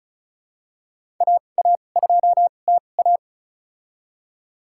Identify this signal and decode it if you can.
Morse code, heard as a keyed tone, about 34 wpm, spaced as for 25 wpm, 700 Hz. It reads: AA2TA
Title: USA Call Signs - 5 Characters - 2x Character Spacing - 35wpm